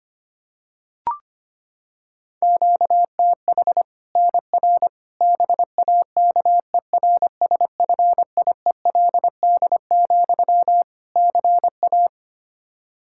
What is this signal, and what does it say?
Morse code, heard as a keyed tone, about 25 wpm, 700 Hz. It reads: QTH NR BAKERSFIELD, CA
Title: K1USN SST QSO Practice 25wpm